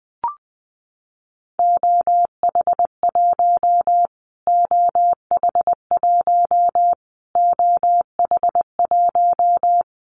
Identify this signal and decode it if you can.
Morse code, heard as a keyed tone, about 20 wpm, 700 Hz. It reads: OH1 OH1 OH1
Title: Callsign Prefixes for Top 25 DXCC - Repeated 3x - 20wpm